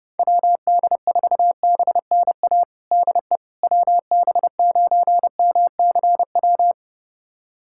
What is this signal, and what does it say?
Morse code, heard as a keyed tone, about 29 wpm, 700 Hz. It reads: WD4BNA DE WB9MCW
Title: USA Callsign DE Callsign - 6 Characters - 30wpm